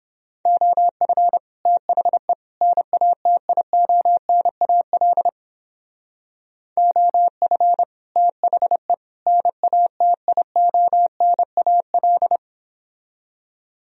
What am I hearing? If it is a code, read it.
Morse code, heard as a keyed tone, about 31 wpm, 700 Hz. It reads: OF THE NATIONAL OF THE NATIONAL